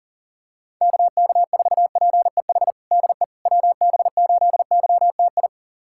Morse code, heard as a keyed tone, about 37 wpm, 700 Hz. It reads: KK4PEH DE WB8YTI
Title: USA Callsign DE Callsign - 6 Characters - 40wpm